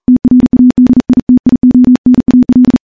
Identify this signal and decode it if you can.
Morse code, heard as a keyed tone, about 38 wpm, 250 Hz. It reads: TRADITIONAL
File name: morse-training-40WPM